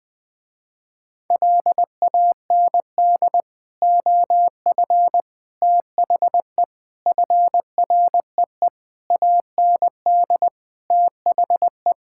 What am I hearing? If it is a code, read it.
Morse code, heard as a keyed tone, about 20 wpm, 700 Hz. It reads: LAND OF THE FREE AND THE
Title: Sentences with 9-Word Phrases - Spoken Beginning - 20wpm